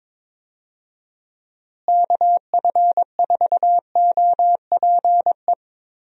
Morse code, heard as a keyed tone, about 22 wpm, 700 Hz. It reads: KF4OPE
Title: USA Call Signs - 6 Characters - Repeated 2x - 22wpm